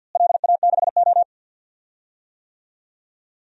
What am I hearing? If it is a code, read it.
Morse code, heard as a keyed tone, about 48 wpm, 700 Hz. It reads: LA6O
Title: DXCC-100 Call Signs - 4 Characters - Repeated 3x - 50wpm